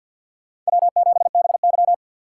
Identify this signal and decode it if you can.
Morse code, heard as a keyed tone, about 62 wpm, 700 Hz. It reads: W7BY
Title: Speed Racing - USA 1x2 Call Signs 50wpm